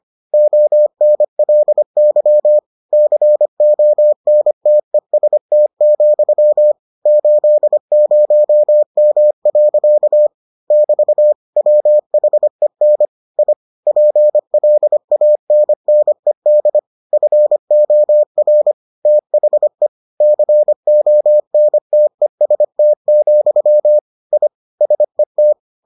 Morse code, heard as a keyed tone, about 25 wpm, 600 Hz. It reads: ONLY CONTEST, 80M. <BT> WHEN I PLANNED FOR THE CONTEST, I SET